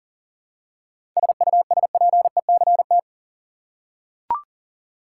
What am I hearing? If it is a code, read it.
Morse code, heard as a keyed tone, about 38 wpm, 700 Hz. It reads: SUSPECT
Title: Seven Letter Words - 40wpm